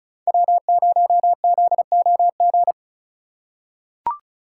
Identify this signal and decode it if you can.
Morse code, heard as a keyed tone, about 34 wpm, 700 Hz. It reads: W0ZOG
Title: USA Call Signs - 5 Characters - 4x Character Spacing - 35wpm